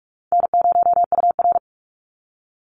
Morse code, heard as a keyed tone, about 44 wpm, 700 Hz. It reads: N0UR
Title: Easy Call Signs Repeated 2x - 45wpm